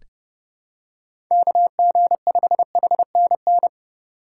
Morse code, heard as a keyed tone, about 32 wpm, 700 Hz. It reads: KG5HNN
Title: USA Call Signs - 6 Characters - 3x Character Spacing - 30wpm